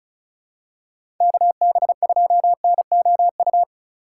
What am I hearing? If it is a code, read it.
Morse code, heard as a keyed tone, about 36 wpm, 690 Hz. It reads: KB2NOU
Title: USA Call Signs - 6 Characters - 4x Character Spacing - 35wpm